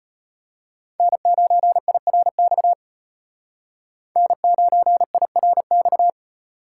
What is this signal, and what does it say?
Morse code, heard as a keyed tone, about 32 wpm, 700 Hz. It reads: N9IRX N9IRX